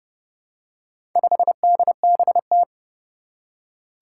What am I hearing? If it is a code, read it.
Morse code, heard as a keyed tone, about 30 wpm, 700 Hz. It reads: 5DBT
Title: Four Challenging Letter-Number Combinations 30wpm